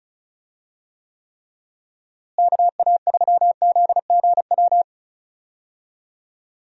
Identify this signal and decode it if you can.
Morse code, heard as a keyed tone, about 34 wpm, 700 Hz. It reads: KA3ZGW
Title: USA Call Signs - 6 Characters - Repeated 3x - 35wpm